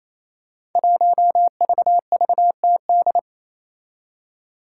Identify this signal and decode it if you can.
Morse code, heard as a keyed tone, about 28 wpm, 700 Hz. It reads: 1VVTD
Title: Five Challenging Letter-Number Combinations 28wpm